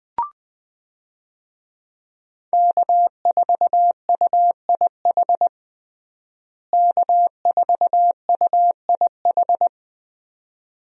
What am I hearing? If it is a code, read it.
Morse code, heard as a keyed tone, about 20 wpm, 700 Hz. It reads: K4UIH K4UIH
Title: USA Call Signs - 5 Characters - Repeated 2x - 20wpm